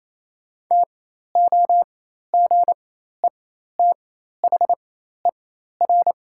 Morse code, heard as a keyed tone, about 28 wpm, spaced as for 13 wpm, 690 Hz. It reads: TOGETHER
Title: 4x Character Spacing - Top 300 Words - 28wpm